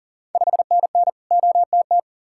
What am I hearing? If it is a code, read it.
Morse code, heard as a keyed tone, about 41 wpm, 690 Hz. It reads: 5NN OTT
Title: Ontario QSO Party Exchange 40wpm